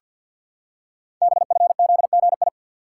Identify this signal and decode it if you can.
Morse code, heard as a keyed tone, about 50 wpm, 700 Hz. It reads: BFZGI
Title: Code Groups - 5 Letters - 50wpm